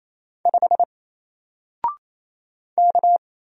Morse code, heard as a keyed tone, about 27 wpm, 700 Hz. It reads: 5 K
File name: Single-Letter-Number-28wpm